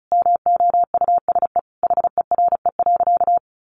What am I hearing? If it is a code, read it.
Morse code, heard as a keyed tone, about 34 wpm, 700 Hz. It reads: MOUSE HERE.
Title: A Sound Of Thunder 35wpm